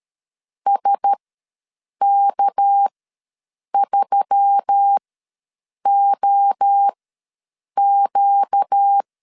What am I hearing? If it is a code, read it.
Morse code, heard as a keyed tone, about 13 wpm, spaced as for 7 wpm, 800 Hz. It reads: SK3OQ
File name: LEZIONI / 6 PKX0 / 6 PKX ws 12wpm 3106